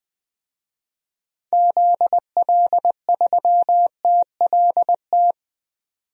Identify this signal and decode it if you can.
Morse code, heard as a keyed tone, about 20 wpm, 700 Hz. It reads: ZL3TLT
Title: DXCC-100 Call Signs - 6 Characters - 20wpm